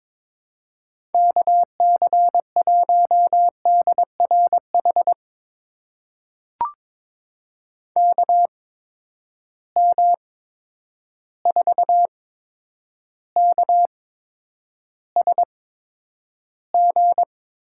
Morse code, heard as a keyed tone, about 22 wpm, spaced as for 9 wpm, 700 Hz. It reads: KC1DRH KM4KSG
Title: USA Call Signs - 6 Characters - 8x Character Spacing - 22wpm